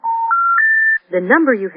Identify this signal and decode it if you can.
Special information tone, with speech.